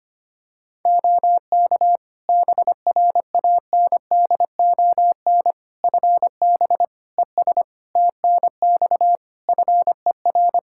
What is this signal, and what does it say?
Morse code, heard as a keyed tone, about 25 wpm, 700 Hz. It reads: OK BRANDON FB ES TNX FER
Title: K1USN SST QSO Practice 25wpm